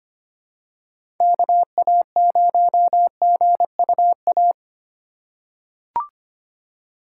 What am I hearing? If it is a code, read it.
Morse code, heard as a keyed tone, about 25 wpm, 700 Hz. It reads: KA0GUA
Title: USA Call Signs - 6 Characters - 2x Character Spacing - 25wpm